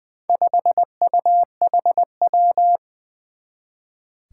Morse code, heard as a keyed tone, about 20 wpm, 700 Hz. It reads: 5UHW
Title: Four Challenging Letter-Number Combinations 20wpm